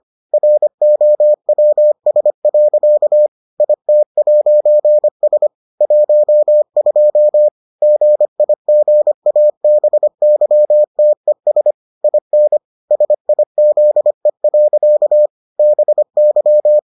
Morse code, heard as a keyed tone, about 25 wpm, 600 Hz. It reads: ROWS. IT'S 12 GIGABYTES IN SIZE. BY